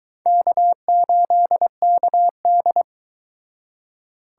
Morse code, heard as a keyed tone, about 22 wpm, 700 Hz. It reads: K8KD